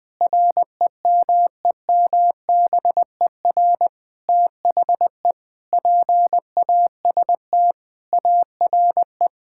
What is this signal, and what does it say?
Morse code, heard as a keyed tone, about 20 wpm, 700 Hz. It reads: REMEMBER THE PAST ARE
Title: Sentences with 8-Word Phrases - 20wpm